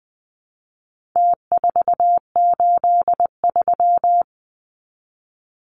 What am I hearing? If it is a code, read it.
Morse code, heard as a keyed tone, about 20 wpm, 700 Hz. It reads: T483
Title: Numbers - Contest Style - 4-Digit 20wpm